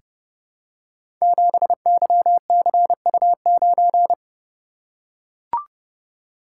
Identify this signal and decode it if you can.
Morse code, heard as a keyed tone, about 31 wpm, 700 Hz. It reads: 7YCU9
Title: Code Groups - 5 Letter:Numbers - 30wpm